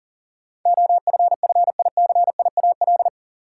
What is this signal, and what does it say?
Morse code, heard as a keyed tone, about 40 wpm, 700 Hz. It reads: OFFICIAL
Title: Sentences from Top 1000 Words - New Word Drill - 40wpm